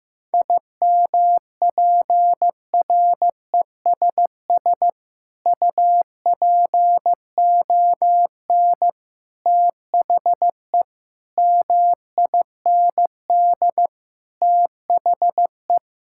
Morse code, heard as a keyed tone, about 15 wpm, 700 Hz. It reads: IMPRESS UPON THE MIND THE
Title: Sentences from Top 1600 Words 15wpm